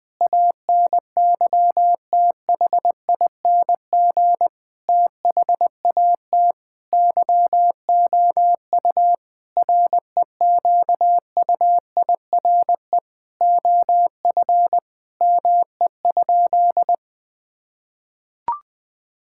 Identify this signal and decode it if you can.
Morse code, heard as a keyed tone, about 20 wpm, 700 Hz. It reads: ANYTHING THAT YOU REQUIRE OF ME?
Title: Sentences from Top 1000 Words - Review Part 6 - 20wpm